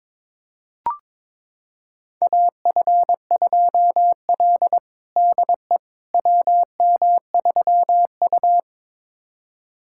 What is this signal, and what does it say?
Morse code, heard as a keyed tone, about 22 wpm, 700 Hz. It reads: AF2L DE WM3U
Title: USA Callsign DE Callsign - 4 Characters - 22wpm